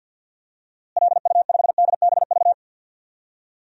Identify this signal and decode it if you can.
Morse code, heard as a keyed tone, about 53 wpm, 700 Hz. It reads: LU5DBV
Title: DXCC-100 Call Signs - 6 Characters - Repeated 3x - 50wpm